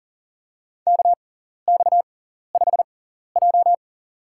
Morse code, heard as a keyed tone, about 40 wpm, spaced as for 14 wpm, 700 Hz. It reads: KX5J
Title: USA Call Signs - 6x Character Spacing - 40wpm